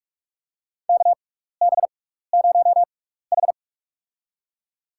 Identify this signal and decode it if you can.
Morse code, heard as a keyed tone, about 46 wpm, spaced as for 16 wpm, 700 Hz. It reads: KB0H
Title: USA Call Signs - 4 Characters - 6x Character Spacing - 45wpm